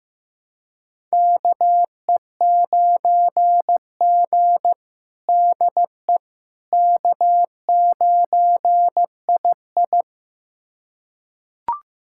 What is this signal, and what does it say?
Morse code, heard as a keyed tone, about 15 wpm, 700 Hz. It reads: KE9G DE K9II
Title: USA Callsign DE Callsign - 4 Characters - 15wpm